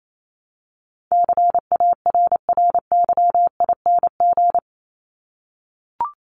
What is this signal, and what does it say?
Morse code, heard as a keyed tone, about 28 wpm, 700 Hz. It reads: CARRYING
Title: ING Suffix - Top 100 Words with Base Words 28wpm